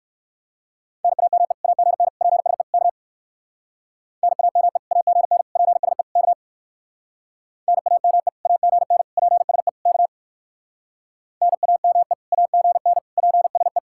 Morse code, heard as a keyed tone, about 63 wpm, 700 Hz. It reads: NAME AGN PSE K NAME AGN PSE K NAME AGN PSE K NAME AGN PSE